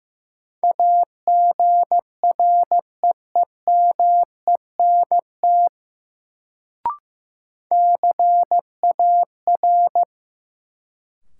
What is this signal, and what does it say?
Morse code, heard as a keyed tone, about 15 wpm, 700 Hz. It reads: AGREEMENT CAR
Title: Sets of 5 Words Encore - New Word Drill - 15wpm